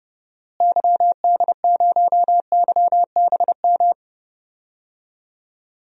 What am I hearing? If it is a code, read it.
Morse code, heard as a keyed tone, about 30 wpm, 700 Hz. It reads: YD0YBM
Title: DXCC-100 Call Signs - 6 Characters - Repeated 3x - 30wpm